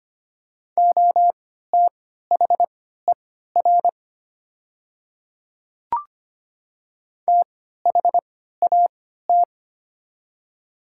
Morse code, heard as a keyed tone, about 25 wpm, spaced as for 14 wpm, 700 Hz. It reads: OTHER THAT
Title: 3x Character Spacing - Top 100 Words - 25wpm